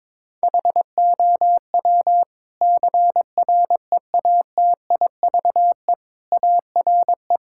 Morse code, heard as a keyed tone, about 21 wpm, 700 Hz. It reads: HOW CREATIVE ARE
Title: Sentences from Top 1000 Words - Review Part 5 - 22wpm